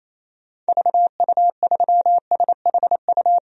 Morse code, heard as a keyed tone, about 28 wpm, 700 Hz. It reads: VU3SHU
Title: DXCC-100 Call Signs - 6 Characters - Repeated 2x - 28wpm